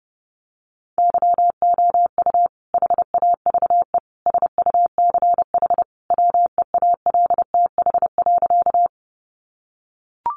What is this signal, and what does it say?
Morse code, heard as a keyed tone, about 30 wpm, 700 Hz. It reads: YOU HAVE SUCH WEALTH.
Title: Sentences from Top 1200 Words 30wpm